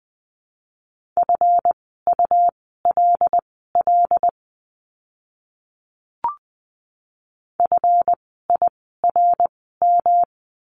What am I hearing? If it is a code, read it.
Morse code, heard as a keyed tone, about 20 wpm, spaced as for 15 wpm, 700 Hz. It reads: FULL FIRM
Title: 139 Lesson 23f - Letter f - Words 20wpm